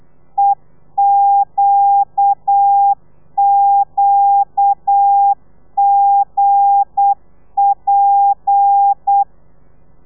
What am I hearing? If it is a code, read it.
Morse code, heard as a keyed tone, about 8 wpm, 800 Hz. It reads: EQQGP
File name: telegrafia / i2tfj / Lez9 35